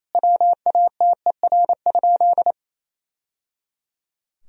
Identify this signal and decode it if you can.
Morse code, heard as a keyed tone, about 28 wpm, 700 Hz. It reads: WATER?
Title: Sentences from Top 1000 Words - Review Part 3 - 28wpm